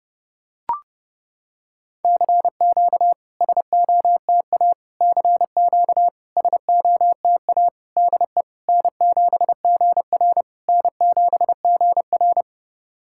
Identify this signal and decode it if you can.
Morse code, heard as a keyed tone, about 30 wpm, 700 Hz. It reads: CQ SOTA CQ SOTA DE N7GR N7GR
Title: SOTA QSO Practice 30wpm